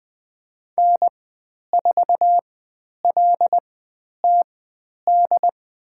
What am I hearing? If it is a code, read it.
Morse code, heard as a keyed tone, about 20 wpm, spaced as for 10 wpm, 700 Hz. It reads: N4LTD
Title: 102 Lesson 17g - Letter u - Call Signs 20wpm